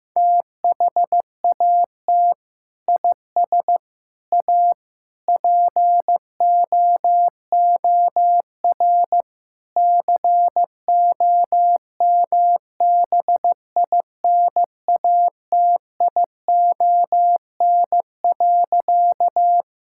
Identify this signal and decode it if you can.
Morse code, heard as a keyed tone, about 15 wpm, 700 Hz. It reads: THAT IS A POOR COMBINATION.
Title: Sentences from Top 1000 Words - Review Part 6 - 15wpm